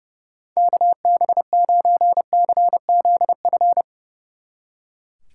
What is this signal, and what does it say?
Morse code, heard as a keyed tone, about 36 wpm, 700 Hz. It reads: KB9CZF
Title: Speed Racing - USA 2x3 Call Signs 30wpm